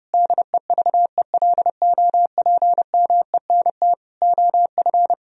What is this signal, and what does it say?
Morse code, heard as a keyed tone, about 29 wpm, 700 Hz. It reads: DEVELOPMENT OF